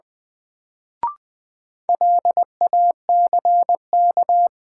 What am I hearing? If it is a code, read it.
Morse code, heard as a keyed tone, about 20 wpm, 700 Hz. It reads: LACK
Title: Sets of 3 Words - New Word Drill - 20wpm